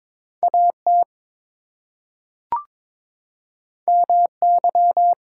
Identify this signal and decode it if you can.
Morse code, heard as a keyed tone, about 22 wpm, 700 Hz. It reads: AT MY
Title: Two Letter Words - 22wpm